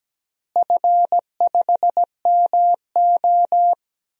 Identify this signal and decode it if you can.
Morse code, heard as a keyed tone, about 17 wpm, 700 Hz. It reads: F5MO
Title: DXCC-100 Call Signs - 4 Characters - Repeated 2x - 17wpm